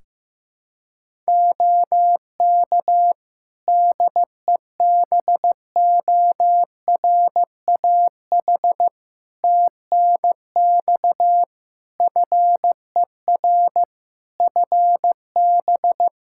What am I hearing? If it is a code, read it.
Morse code, heard as a keyed tone, about 15 wpm, 700 Hz. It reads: OK DEBORAH TNX FER FB
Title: Practice QSO - Exchange 4 - Ending - 15wpm